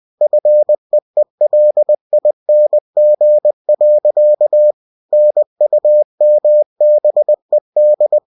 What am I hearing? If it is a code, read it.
Morse code, heard as a keyed tone, about 20 wpm, 600 Hz. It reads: FEELING. NUMBED
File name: cw-books / winnie-the-pooh / winnie-20wpm-page-0088